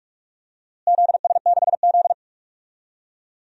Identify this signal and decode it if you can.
Morse code, heard as a keyed tone, about 47 wpm, 700 Hz. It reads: ZS6Z
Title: DXCC-100 Call Signs - 4 Characters - Repeated 3x - 45wpm